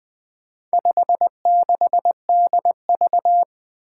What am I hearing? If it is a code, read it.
Morse code, heard as a keyed tone, about 20 wpm, 700 Hz. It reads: 56DV
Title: Four Challenging Letter-Number Combinations 20wpm